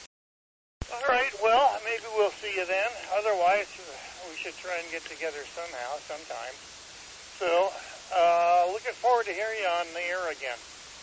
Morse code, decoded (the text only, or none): none